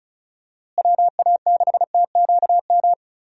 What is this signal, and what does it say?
Morse code, heard as a keyed tone, about 34 wpm, 700 Hz. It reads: WA6TQM
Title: USA Call Signs - 6 Characters - Repeated 2x - 35wpm